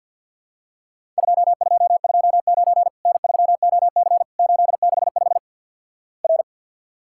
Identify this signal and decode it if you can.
Morse code, heard as a keyed tone, about 50 wpm, 700 Hz. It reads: 2229 N3OC 865 R
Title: Worked All Europe DX Contest - QTC Exchange with Repeats - 50wpm